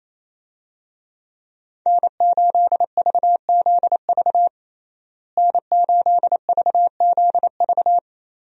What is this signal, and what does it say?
Morse code, heard as a keyed tone, about 28 wpm, 700 Hz. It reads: N8VZV N8VZV
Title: USA Call Signs - 5 Characters - Repeated 3x - 28wpm